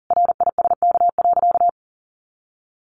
Morse code, heard as a keyed tone, about 40 wpm, 700 Hz. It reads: RISK.
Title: Sentences from Top 600 Words 40wpm